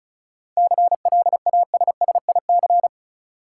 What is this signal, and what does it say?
Morse code, heard as a keyed tone, about 35 wpm, 700 Hz. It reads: CLASSIC
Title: Sentences from Top 1000 Words - New Word Drill - 35wpm